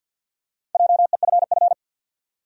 Morse code, heard as a keyed tone, about 53 wpm, 700 Hz. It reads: JEFF
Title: USA Common Names - 50wpm